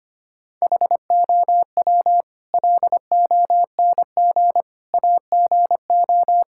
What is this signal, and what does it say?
Morse code, heard as a keyed tone, about 25 wpm, 700 Hz. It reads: HOW LONG AGO
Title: Sentences from Top 1000 Words - Review Part 6 - 25wpm